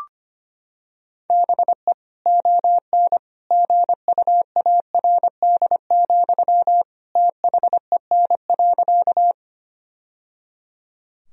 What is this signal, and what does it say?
Morse code, heard as a keyed tone, about 25 wpm, 700 Hz. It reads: BE ON GUARD, THEN.
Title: Sentences from Top 1000 Words - Review Part 4 - 25wpm